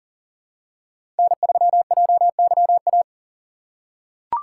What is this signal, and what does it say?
Morse code, heard as a keyed tone, about 40 wpm, 700 Hz. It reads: N3JYA
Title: USA Call Signs - 2x Character Spacing - 40wpm